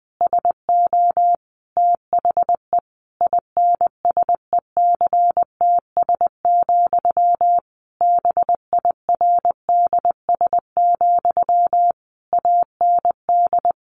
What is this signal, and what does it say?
Morse code, heard as a keyed tone, about 20 wpm, 700 Hz. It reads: SO THE INSECTS, BIRDS, AND
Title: A Sound Of Thunder 20wpm